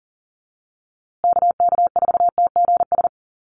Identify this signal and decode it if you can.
Morse code, heard as a keyed tone, about 40 wpm, 700 Hz. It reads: KK4TGS
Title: USA Call Signs Repeated 3x 40wpm